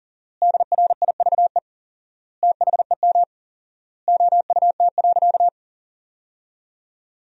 Morse code, heard as a keyed tone, about 40 wpm, 700 Hz. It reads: DRIVE THEM OUT.
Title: 4x Word-Spacing - Sentences from Top 500 Words - 40wpm